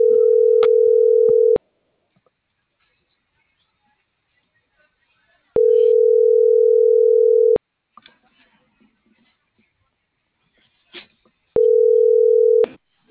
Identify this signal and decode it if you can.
Ringback tone (North America), 440 + 480 Hz.